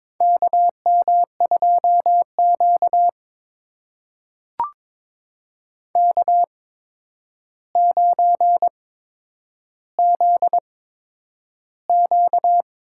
Morse code, heard as a keyed tone, about 23 wpm, spaced as for 10 wpm, 700 Hz. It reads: KM2Q K9ZQ
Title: USA Call Signs - 4 Characters - 8x Character Spacing - 22wpm